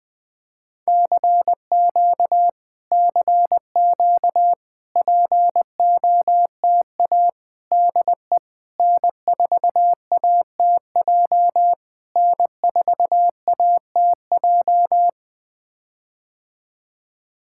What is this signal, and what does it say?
Morse code, heard as a keyed tone, about 20 wpm, 700 Hz. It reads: CQ CQ POTA DE N4ATJ N4ATJ
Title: POTA QSO Practice - 20wpm